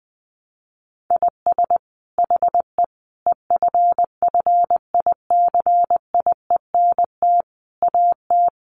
Morse code, heard as a keyed tone, about 20 wpm, 700 Hz. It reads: IS HE EFFICIENT AT
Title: Sentences from Top 1300 Words 20wpm